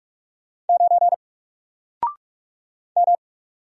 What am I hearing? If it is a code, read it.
Morse code, heard as a keyed tone, about 48 wpm, 700 Hz. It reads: 9 M
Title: Single Letter-Number Alt 45wpm